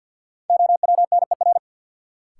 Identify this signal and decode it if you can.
Morse code, heard as a keyed tone, about 49 wpm, 700 Hz. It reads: OWNER
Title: Sentences from Top 1600 Words - New Word Drill Alt - 50wpm